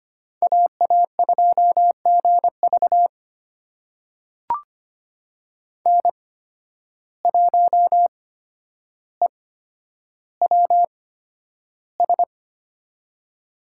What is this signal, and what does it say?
Morse code, heard as a keyed tone, about 25 wpm, spaced as for 11 wpm, 700 Hz. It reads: AA2GV N1EWS
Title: USA Call Signs - 5 Characters - 8x Character Spacing - 25wpm